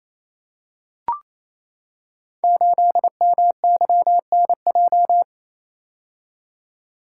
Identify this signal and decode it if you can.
Morse code, heard as a keyed tone, about 29 wpm, 700 Hz. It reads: 8MYNJ
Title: Code Groups - 5 Letter:Numbers - 28wpm